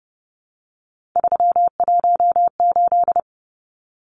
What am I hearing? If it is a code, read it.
Morse code, heard as a keyed tone, about 30 wpm, 700 Hz. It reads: 318
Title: Numbers - 3-Digit 30wpm